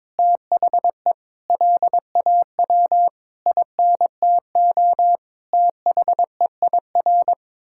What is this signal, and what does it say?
Morse code, heard as a keyed tone, about 22 wpm, 700 Hz. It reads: THE LAW INTO THEIR
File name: practice / 22wpm / Sentences with 8-Word Phrases - 22wpm